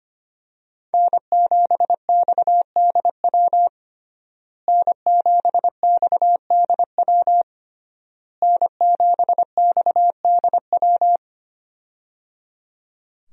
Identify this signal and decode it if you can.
Morse code, heard as a keyed tone, about 25 wpm, 700 Hz. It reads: N7XDW N7XDW N7XDW
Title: USA Call Signs - 5 Characters - Repeated 3x - 25wpm